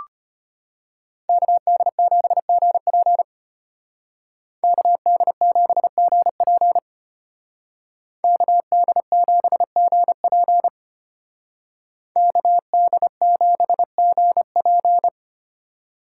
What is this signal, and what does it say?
Morse code, heard as a keyed tone, about 30 wpm, 700 Hz. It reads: KD7GP KD7GP KD7GP KD7GP
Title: Speed Racing - USA 2x2 Call Signs 25wpm